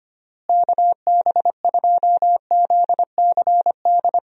Morse code, heard as a keyed tone, about 25 wpm, 700 Hz. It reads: KB2ZCD
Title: USA Call Signs - 6 Characters - Repeated 2x - 25wpm